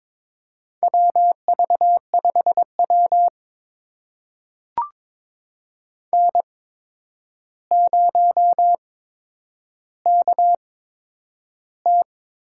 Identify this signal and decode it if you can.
Morse code, heard as a keyed tone, about 22 wpm, spaced as for 9 wpm, 700 Hz. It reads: WV5W N0KT
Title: USA Call Signs - 4 Characters - 8x Character Spacing - 22wpm